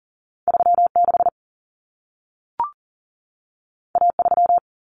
Morse code, heard as a keyed tone, about 40 wpm, 700 Hz. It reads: 36 A3
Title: Common Cut Numbers - 2-Digit 40wpm